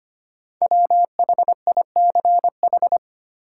Morse code, heard as a keyed tone, about 25 wpm, 700 Hz. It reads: WHICH
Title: Sentences from Top 100 Words - New Word Drill - 25wpm